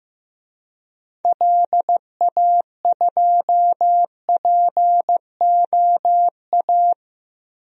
Morse code, heard as a keyed tone, about 14 wpm, 700 Hz. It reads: LA2POA
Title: DXCC-100 Call Signs - 6 Characters - Repeated 2x - 15wpm